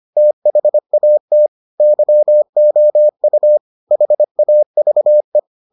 Morse code, heard as a keyed tone, about 25 wpm, 600 Hz. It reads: THAT YOU HAVE